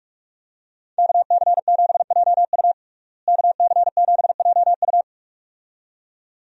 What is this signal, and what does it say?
Morse code, heard as a keyed tone, about 41 wpm, 700 Hz. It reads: KC7JU KC7JU
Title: USA Call Signs - 5 Characters - Repeated 2x - 45wpm